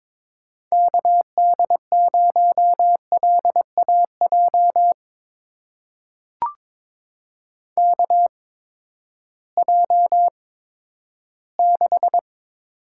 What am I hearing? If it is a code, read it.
Morse code, heard as a keyed tone, about 22 wpm, spaced as for 13 wpm, 700 Hz. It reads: KD0LAJ KJ6
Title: USA Call Signs - 6 Characters - 8x Character Spacing - 22wpm